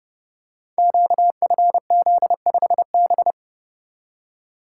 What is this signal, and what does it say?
Morse code, heard as a keyed tone, about 28 wpm, 700 Hz. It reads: QFZ5B
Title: Code Groups - 5 Letter:Numbers - 30wpm